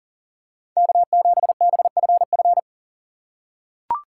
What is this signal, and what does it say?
Morse code, heard as a keyed tone, about 40 wpm, 700 Hz. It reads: K7BFF
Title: USA Call Signs - 4x Character Spacing - 40wpm